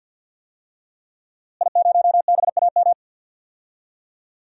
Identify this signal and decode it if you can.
Morse code, heard as a keyed tone, about 52 wpm, 700 Hz. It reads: I0BAM
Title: DXCC-100 Call Signs - 5 Characters - Repeated 3x - 50wpm